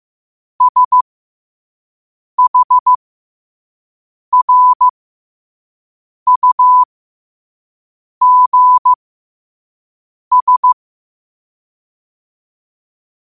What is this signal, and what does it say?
Morse code, heard as a keyed tone, about 15 wpm, spaced as for 5 wpm, 1000 Hz. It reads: SHRUGS
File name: qotdfiles / 5wpm